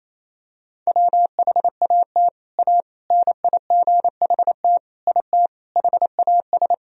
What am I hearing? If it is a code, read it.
Morse code, heard as a keyed tone, about 28 wpm, 700 Hz. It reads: WHAT A NIGHT IT HAS
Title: Sentences from Top 300 Words and Abbreviations 28wpm